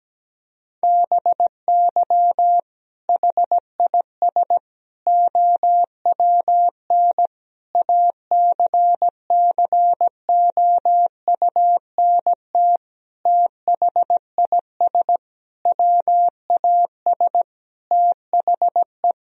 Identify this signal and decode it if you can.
Morse code, heard as a keyed tone, about 17 wpm, 700 Hz. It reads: BY HIS OWN ACCOUNT THIS WAS THE
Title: Sentences from Top 1000 Words - Review Part 1 - 17wpm